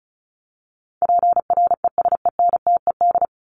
Morse code, heard as a keyed tone, about 34 wpm, 700 Hz. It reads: PRESENTED
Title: ED Suffix - Top 100 Words with Base Words 35wpm